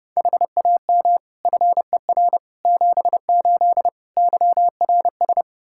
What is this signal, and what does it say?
Morse code, heard as a keyed tone, about 30 wpm, 700 Hz. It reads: HAM FER 78 YRS
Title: Practice QSO - Exchange 3 - More Info - 30wpm